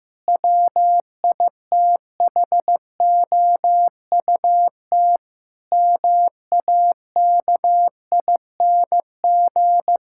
Morse code, heard as a keyed tone, about 15 wpm, 700 Hz. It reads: WITHOUT MAKING
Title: Sentences from Top 1000 Words - Review Part 6 - 15wpm